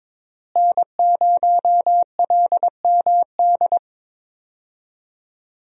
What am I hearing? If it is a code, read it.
Morse code, heard as a keyed tone, about 22 wpm, 700 Hz. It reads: N0LMD
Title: USA Call Signs - 5 Characters - Repeated 2x - 22wpm